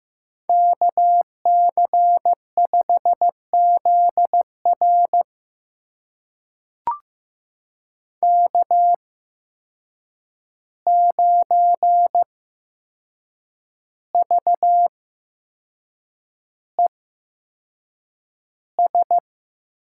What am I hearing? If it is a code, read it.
Morse code, heard as a keyed tone, about 15 wpm, spaced as for 6 wpm, 700 Hz. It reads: KC5ZR K9VES
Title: USA Call Signs - 5 Characters - 8x Character Spacing - 15wpm